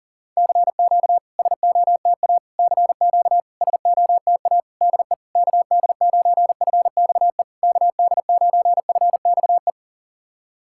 Morse code, heard as a keyed tone, about 40 wpm, 690 Hz. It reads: CQ SOTA CQ SOTA DE KD9FXE KD9FXE
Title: SOTA QSO Practice 40wpm